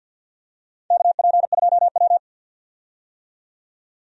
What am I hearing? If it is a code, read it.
Morse code, heard as a keyed tone, about 53 wpm, 700 Hz. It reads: KP2W
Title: DXCC-100 Call Signs - 4 Characters - 50wpm